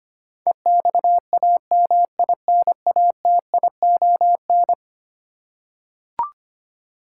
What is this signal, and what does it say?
Morse code, heard as a keyed tone, about 25 wpm, 700 Hz. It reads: EXAMINATION
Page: morse-code-ninja-practice-25wpm